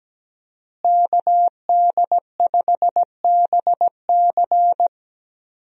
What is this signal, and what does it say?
Morse code, heard as a keyed tone, about 17 wpm, 700 Hz. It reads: KD5BC
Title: USA Call Signs - 5 Characters - Repeated 2x - 17wpm